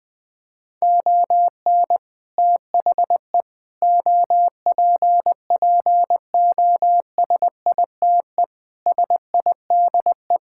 Morse code, heard as a keyed tone, about 21 wpm, 700 Hz. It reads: ON THE OPPOSITE SIDE
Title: Sets of 6 Words - 20wpm